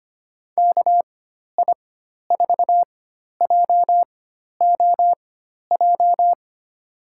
Morse code, heard as a keyed tone, about 25 wpm, spaced as for 12 wpm, 700 Hz. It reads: KI4JOJ
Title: USA Call Signs - 6 Characters - 4x Character Spacing - 25wpm